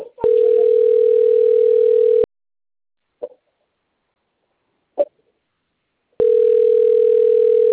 Ringback tone (North America), 440 + 480 Hz.
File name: out- - - - - .35417